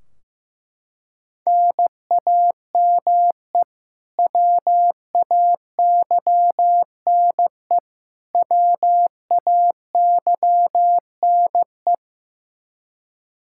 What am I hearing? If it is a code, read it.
Morse code, heard as a keyed tone, about 16 wpm, 700 Hz. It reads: NAME WAYNE WAYNE
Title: Practice QSO - Exchange 1 - Beginning - 15wpm